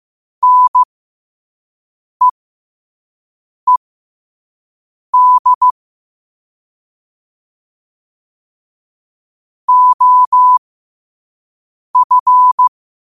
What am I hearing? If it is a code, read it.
Morse code, heard as a keyed tone, about 15 wpm, spaced as for 5 wpm, 1000 Hz. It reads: NEED OF
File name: qotdfiles / 5wpm